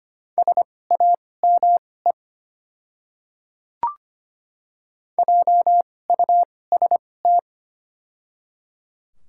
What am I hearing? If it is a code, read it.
Morse code, heard as a keyed tone, about 25 wpm, spaced as for 18 wpm, 700 Hz. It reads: SAME JUST
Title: 2x Character Spacing - Top 200 Words - 25wpm